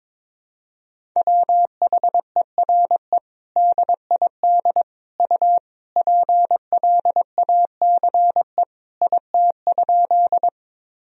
Morse code, heard as a keyed tone, about 21 wpm, 700 Hz. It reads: WHERE DID U PLACE IT?
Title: Sentences from Top 200 Words and Abbreviations 22wpm